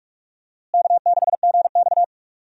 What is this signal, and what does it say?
Morse code, heard as a keyed tone, about 47 wpm, 700 Hz. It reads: K6GX
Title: USA Call Signs - 4 Characters - Repeated 2x - 45wpm